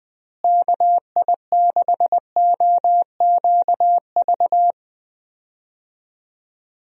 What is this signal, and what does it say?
Morse code, heard as a keyed tone, about 21 wpm, 700 Hz. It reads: KI6OQV
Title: USA Call Signs - 6 Characters - Repeated 2x - 20wpm